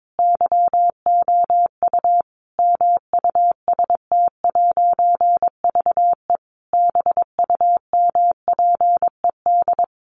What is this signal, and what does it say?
Morse code, heard as a keyed tone, about 22 wpm, 700 Hz. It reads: YOU MUST'VE BUMPED
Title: A Sound Of Thunder 22wpm